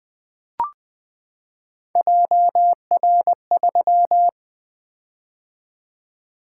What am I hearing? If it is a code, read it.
Morse code, heard as a keyed tone, about 20 wpm, 700 Hz. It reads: JR3
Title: Three Letter-Number Combinations - 20wpm